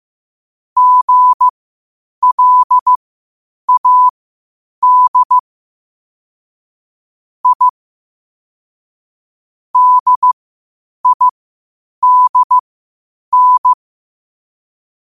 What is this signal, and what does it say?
Morse code, heard as a keyed tone, about 15 wpm, spaced as for 8 wpm, 1000 Hz. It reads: GLAD I DIDN
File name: qotdfiles / 8wpm